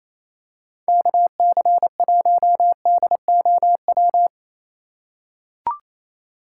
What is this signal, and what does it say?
Morse code, heard as a keyed tone, about 28 wpm, 700 Hz. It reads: KC1DOW
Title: USA Call Signs - 6 Characters - 6x Character Spacing - 28wpm